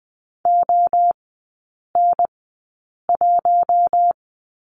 Morse code, heard as a keyed tone, about 20 wpm, spaced as for 8 wpm, 700 Hz. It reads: ON1
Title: 053 Lesson 10e - Letter r - Three Characters 20wpm